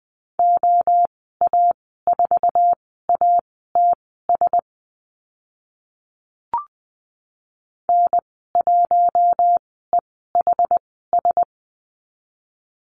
Morse code, heard as a keyed tone, about 20 wpm, spaced as for 14 wpm, 700 Hz. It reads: OA4ATS N1EHS
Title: 069 Lesson 12g - Letter d - Call Signs 20wpm